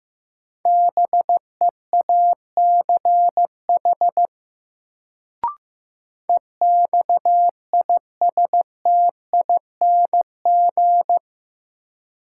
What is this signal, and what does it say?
Morse code, heard as a keyed tone, about 15 wpm, 700 Hz. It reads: BEACH EXISTING
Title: Sentences from Top 1500 Words - New Word Drill - 15wpm